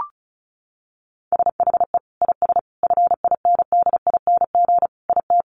Morse code, heard as a keyed tone, about 34 wpm, 700 Hz. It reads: SHE IS FINDING IT
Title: Sentences from Top 1100 Words 35wpm